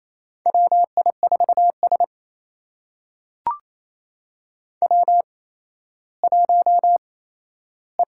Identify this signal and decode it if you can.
Morse code, heard as a keyed tone, about 27 wpm, spaced as for 14 wpm, 700 Hz. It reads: WI4S W1E
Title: USA Call Signs - 8x Character Spacing - 28wpm